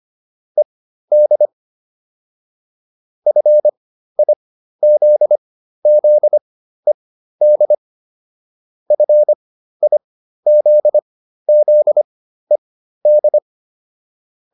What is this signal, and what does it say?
Morse code, heard as a keyed tone, about 25 wpm, spaced as for 13 wpm, 600 Hz. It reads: ED FIZZED FIZZED